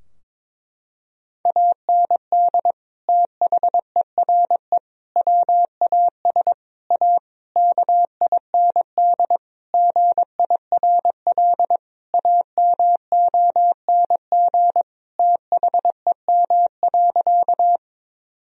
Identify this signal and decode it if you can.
Morse code, heard as a keyed tone, about 22 wpm, 700 Hz. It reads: AND THERE WAS A KIND GIRL AMONG THEM.
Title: Sentences from Top 1000 Words - Review Part 3 - 22wpm